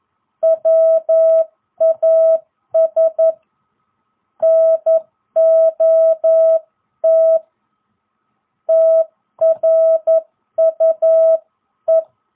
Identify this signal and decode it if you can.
Morse code, heard as a keyed tone, about 11 wpm, 660 Hz. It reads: WAS NOT TRUE